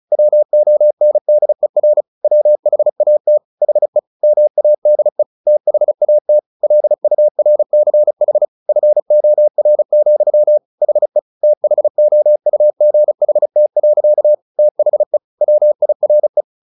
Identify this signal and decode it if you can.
Morse code, heard as a keyed tone, about 34 wpm, 600 Hz. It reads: WONDER WHAT HE MADE THAT LURCH FOR, HE THOUGHT. THE WIRE
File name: the old man and the sea / the old man and the sea 1 35wpm